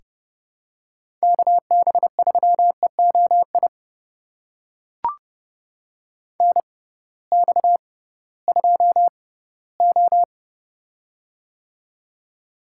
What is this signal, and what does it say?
Morse code, heard as a keyed tone, about 30 wpm, spaced as for 18 wpm, 700 Hz. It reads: KB3EOI NX2O
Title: USA Call Signs - 6x Character Spacing - 30wpm